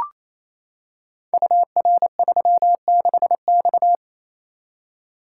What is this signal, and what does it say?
Morse code, heard as a keyed tone, about 27 wpm, 700 Hz. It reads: UR36X
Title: Code Groups - 5 Letter:Numbers - 28wpm